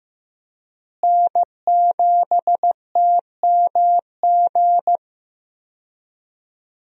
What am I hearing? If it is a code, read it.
Morse code, heard as a keyed tone, about 15 wpm, 700 Hz. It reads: N7TMG
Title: USA Call Signs - 5 Characters - Repeated 3x - 15wpm